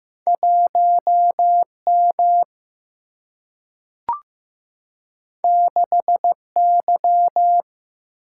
Morse code, heard as a keyed tone, about 15 wpm, 700 Hz. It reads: 1 M 6 Y
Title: Single-Letter Single-Number 15wpm